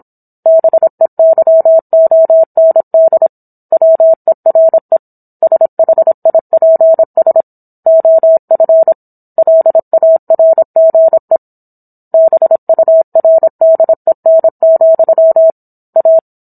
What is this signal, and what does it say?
Morse code, heard as a keyed tone, about 27 wpm, 650 Hz. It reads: BEYOND WERE SHIPS OF LARGE BURDEN, A